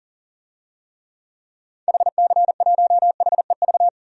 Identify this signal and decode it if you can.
Morse code, heard as a keyed tone, about 40 wpm, 700 Hz. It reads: HC1HEV
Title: DXCC-100 Call Signs - 6 Characters - 40wpm